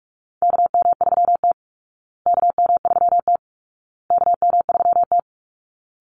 Morse code, heard as a keyed tone, about 44 wpm, 700 Hz. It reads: KM3T KM3T KM3T
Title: Call Signs Repeated 2x - 45wpm